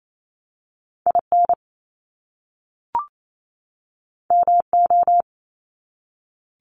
Morse code, heard as a keyed tone, about 28 wpm, 700 Hz. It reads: IN MO
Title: US States 28wpm